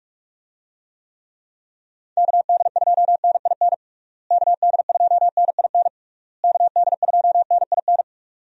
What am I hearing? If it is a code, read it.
Morse code, heard as a keyed tone, about 41 wpm, 700 Hz. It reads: KD2NIN KD2NIN KD2NIN
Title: USA Call Signs - 6 Characters - Repeated 3x - 45wpm